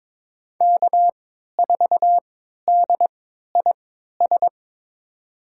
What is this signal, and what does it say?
Morse code, heard as a keyed tone, about 22 wpm, spaced as for 13 wpm, 700 Hz. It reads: K4DIS
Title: USA Call Signs - 5 Characters - 3x Character Spacing - 22wpm